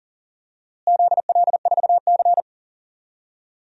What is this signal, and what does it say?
Morse code, heard as a keyed tone, about 38 wpm, 700 Hz. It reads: ZL4C
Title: DXCC-100 Call Signs - 4 Characters - Repeated 2x - 40wpm